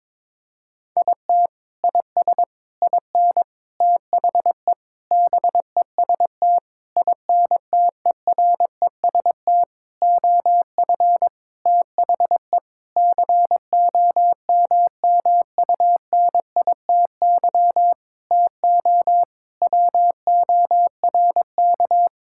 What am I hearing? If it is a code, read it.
Morse code, heard as a keyed tone, about 22 wpm, 700 Hz. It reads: IT IS IN THE BEST INTEREST OF THE COMMUNITY TO WORK
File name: Sentences with 7-Word Phrases - 22wpm